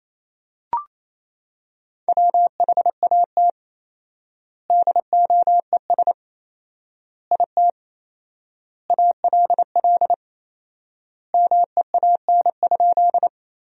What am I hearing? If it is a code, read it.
Morse code, heard as a keyed tone, about 28 wpm, 700 Hz. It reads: WHAT DOES IT ALL MEAN?
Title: 4x Word-Spacing - Sentences from Top 200 Words - 28wpm